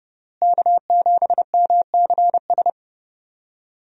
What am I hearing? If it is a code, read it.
Morse code, heard as a keyed tone, about 30 wpm, 700 Hz. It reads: K7MCS
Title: USA Call Signs - 5 Characters - Repeated 3x - 30wpm